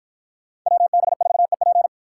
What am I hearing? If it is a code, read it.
Morse code, heard as a keyed tone, about 49 wpm, 700 Hz. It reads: WB4EP